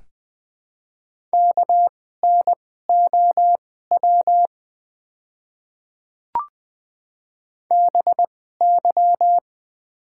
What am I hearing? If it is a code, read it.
Morse code, heard as a keyed tone, about 21 wpm, spaced as for 15 wpm, 700 Hz. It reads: KNOW BY
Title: 2x Character Spacing - Top 500 Words - 20wpm